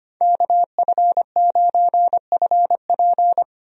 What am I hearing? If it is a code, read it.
Morse code, heard as a keyed tone, about 25 wpm, 700 Hz. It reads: KF9FP
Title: USA Call Signs - 5 Characters - 25wpm